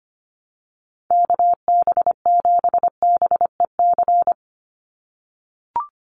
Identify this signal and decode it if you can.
Morse code, heard as a keyed tone, about 25 wpm, 700 Hz. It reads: KB7BEC
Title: USA Call Signs 25wpm